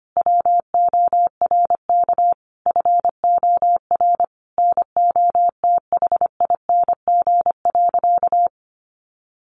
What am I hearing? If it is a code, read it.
Morse code, heard as a keyed tone, about 25 wpm, 700 Hz. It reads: WORK FOR NOTHING.
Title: Sentences from Top 1400 Words 25wpm